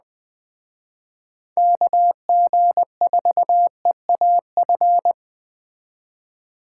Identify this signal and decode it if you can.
Morse code, heard as a keyed tone, about 20 wpm, 700 Hz. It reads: KG4EAF